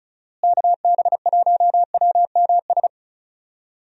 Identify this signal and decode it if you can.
Morse code, heard as a keyed tone, about 34 wpm, 700 Hz. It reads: KB1WMS
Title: USA Call Signs - 6 Characters - Repeated 2x - 35wpm